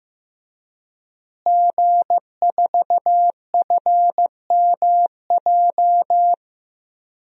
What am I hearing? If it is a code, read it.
Morse code, heard as a keyed tone, about 15 wpm, 700 Hz. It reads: G4FMJ
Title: DXCC-100 Call Signs - 5 Characters - Repeated 3x - 15wpm